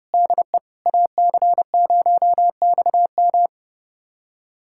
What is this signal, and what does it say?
Morse code, heard as a keyed tone, about 30 wpm, 700 Hz. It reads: DE AC0XM
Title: USA Callsign DE Callsign - 5 Characters - 30wpm